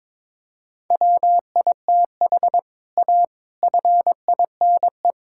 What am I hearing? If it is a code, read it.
Morse code, heard as a keyed tone, about 22 wpm, 700 Hz. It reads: WITH A FINE
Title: Sentences from Top 1000 Words - Review Part 5 - 22wpm